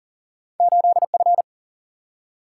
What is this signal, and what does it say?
Morse code, heard as a keyed tone, about 40 wpm, 700 Hz. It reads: 8 F
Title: Single-Letter Single-Number 40wpm